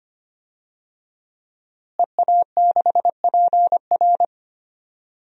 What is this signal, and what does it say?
Morse code, heard as a keyed tone, about 25 wpm, 700 Hz. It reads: EA6PR
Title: DXCC-100 Call Signs - 5 Characters - Repeated 3x - 25wpm